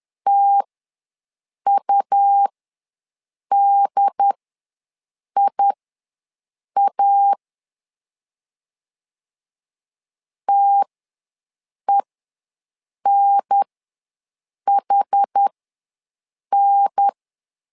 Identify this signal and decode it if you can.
Morse code, heard as a keyed tone, about 10 wpm, spaced as for 6 wpm, 800 Hz. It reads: TUDIA TENHN
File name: 2 UHDN ws 10wpm